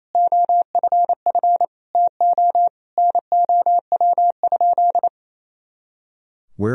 Morse code, heard as a keyed tone, about 28 wpm, 700 Hz. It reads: OFF TO NOW?
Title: Sentences from Top 200 Words and Abbreviations 28wpm